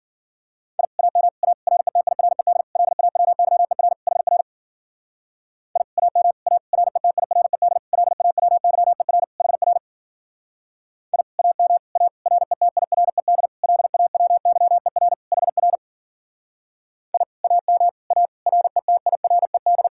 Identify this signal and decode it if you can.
Morse code, heard as a keyed tone, about 52 wpm, 700 Hz. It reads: I AM A RETIRED LAWYER HR I AM A RETIRED LAWYER HR I AM A RETIRED LAWYER HR I AM A RETIRED